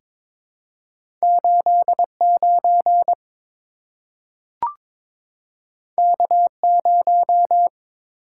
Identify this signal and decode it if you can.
Morse code, heard as a keyed tone, about 22 wpm, 700 Hz. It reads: 89 K0
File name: practice / 22wpm / Two Letter-Number Combinations - 22wpm